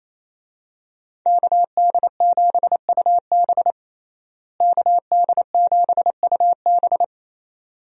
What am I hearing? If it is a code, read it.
Morse code, heard as a keyed tone, about 28 wpm, 700 Hz. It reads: KD7UB KD7UB
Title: USA Call Signs - 5 Characters - Repeated 2x - 28wpm